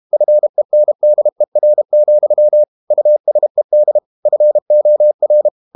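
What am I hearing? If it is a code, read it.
Morse code, heard as a keyed tone, about 32 wpm, 600 Hz. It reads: FENDER, USED FOR
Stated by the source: Morse code practice at 32 wpm from CW med Gnister, reading Ken Follett's novel Never in English.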